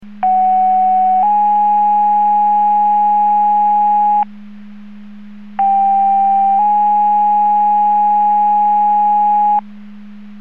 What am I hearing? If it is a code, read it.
Two-tone page, heard as a one-second tone, then a three-second tone, about 750 then 830 Hz, about 790 then 830 Hz. It is heard from 0.2 to 9.6 s.
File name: Portage 154.385 Paging